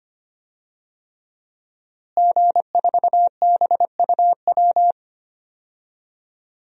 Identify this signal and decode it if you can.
Morse code, heard as a keyed tone, about 25 wpm, 700 Hz. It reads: G4BUW
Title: DXCC-100 Call Signs - 5 Characters - Repeated 3x - 25wpm